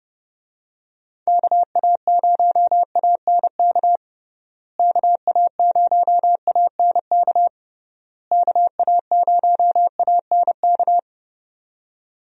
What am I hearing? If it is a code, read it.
Morse code, heard as a keyed tone, about 32 wpm, 700 Hz. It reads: KA0ANK KA0ANK KA0ANK
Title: USA Call Signs - 6 Characters - Repeated 3x - 30wpm